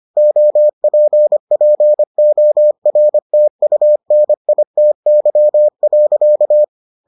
Morse code, heard as a keyed tone, about 25 wpm, 600 Hz. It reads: OPPORTUNITY.